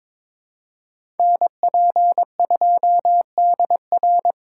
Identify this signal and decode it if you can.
Morse code, heard as a keyed tone, about 22 wpm, 700 Hz. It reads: NP2DR
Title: USA Call Signs - 5 Characters - 4x Character Spacing - 22wpm